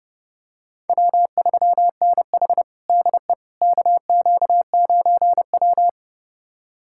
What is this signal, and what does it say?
Morse code, heard as a keyed tone, about 30 wpm, 700 Hz. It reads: W3NH DE KQ9W
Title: USA Callsign DE Callsign - 4 Characters - 30wpm